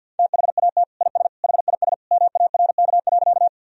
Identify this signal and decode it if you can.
Morse code, heard as a keyed tone, about 51 wpm, 700 Hz. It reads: THAT IS HIS MARK.
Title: Sentences from Top 1000 Words - Review Part 1 - 50wpm